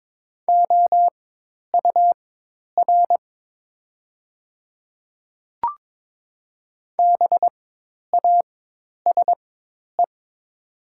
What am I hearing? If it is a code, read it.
Morse code, heard as a keyed tone, about 22 wpm, spaced as for 10 wpm, 700 Hz. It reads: OUR BASE
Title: 4x Character Spacing - Top 500 Words - 22wpm